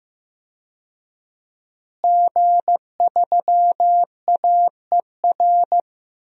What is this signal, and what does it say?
Morse code, heard as a keyed tone, about 15 wpm, 700 Hz. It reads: G3AER
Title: DXCC-100 Call Signs - 5 Characters - Repeated 3x - 15wpm